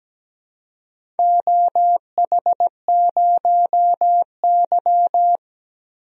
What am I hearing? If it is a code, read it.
Morse code, heard as a keyed tone, about 17 wpm, 700 Hz. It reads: OH0Y
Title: DXCC-100 Call Signs - 4 Characters - 17wpm